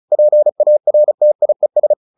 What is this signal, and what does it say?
Morse code, heard as a keyed tone, about 34 wpm, 600 Hz. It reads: PARTIES